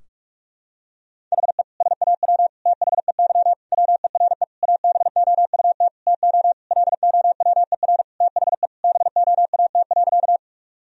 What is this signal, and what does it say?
Morse code, heard as a keyed tone, about 46 wpm, 700 Hz. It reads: HE SAW THEY WERE ABOUT TO LOWER THE BOAT.
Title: Sentences from Top 1000 Words - Review Part 4 - 45wpm